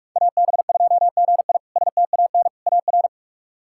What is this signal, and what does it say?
Morse code, heard as a keyed tone, about 46 wpm, 700 Hz. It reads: AB2GI STAN AR
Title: K1USN SST QSO Practice - 45wpm